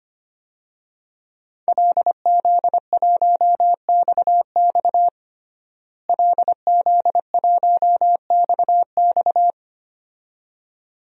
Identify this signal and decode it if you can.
Morse code, heard as a keyed tone, about 25 wpm, 700 Hz. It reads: LZ1XX LZ1XX
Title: DXCC-100 Call Signs - 5 Characters - Repeated 2x - 25wpm